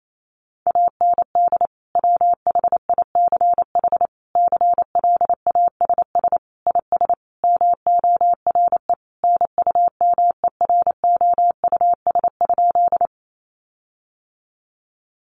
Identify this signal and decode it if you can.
Morse code, heard as a keyed tone, about 28 wpm, 700 Hz. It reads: AND WHICH CLASS IS MORE NUMEROUS?
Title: Sentences from Top 1400 Words 28wpm